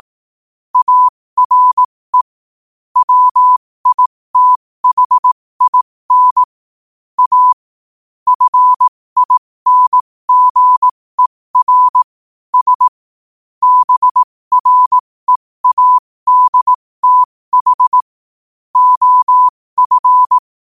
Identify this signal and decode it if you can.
Morse code, heard as a keyed tone, about 18 wpm, spaced as for 15 wpm, 1000 Hz. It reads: ARE WITHIN A FINGER S BREADTH OF
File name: qotdfiles / 15wpm